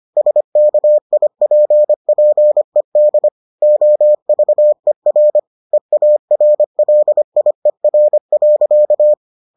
Morse code, heard as a keyed tone, about 25 wpm, 600 Hz. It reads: SKIPPED OVER EARLIER.